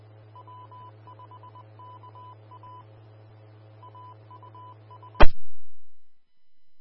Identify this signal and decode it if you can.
Morse code, heard as a keyed tone, about 20 wpm, 1000 Hz. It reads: W5KA AUS
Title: Austin, TX Amateur Repeaters